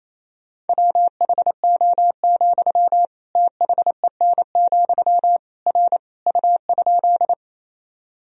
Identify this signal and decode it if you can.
Morse code, heard as a keyed tone, about 28 wpm, 700 Hz. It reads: WHO, THEN, R U?
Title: Sentences from Top 100 Words and Abbreviations 28wpm